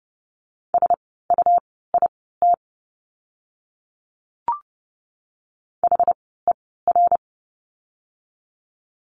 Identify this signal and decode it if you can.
Morse code, heard as a keyed tone, about 30 wpm, spaced as for 17 wpm, 700 Hz. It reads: SUIT HER
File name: Lesson 17f - Letter u - Words 30wpm